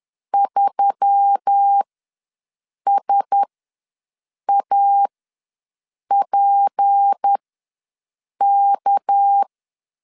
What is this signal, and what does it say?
Morse code, heard as a keyed tone, about 10 wpm, spaced as for 6 wpm, 800 Hz. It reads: 3SAPK
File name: LEZIONI / 6 PKX0 / 6 PKX ws 10wpm 3706